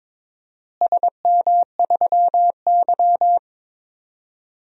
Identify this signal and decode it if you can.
Morse code, heard as a keyed tone, about 22 wpm, 700 Hz. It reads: SM3Y
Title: DXCC-100 Call Signs - 4 Characters - Repeated 2x - 22wpm